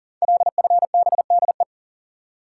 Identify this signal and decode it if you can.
Morse code, heard as a keyed tone, about 38 wpm, 690 Hz. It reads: LFBDE
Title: Code Groups - 5 Letters - 40wpm